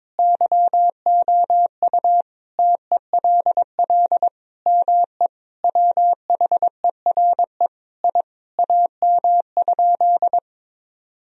Morse code, heard as a keyed tone, about 22 wpm, 700 Hz. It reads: YOU TELL ME WHERE I AM?
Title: Sentences from Top 1000 Words - Review Part 3 - 22wpm